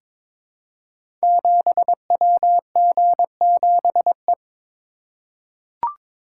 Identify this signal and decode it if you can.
Morse code, heard as a keyed tone, about 23 wpm, 700 Hz. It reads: 7WG7E
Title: Code Groups - 5 Letter:Numbers - 22wpm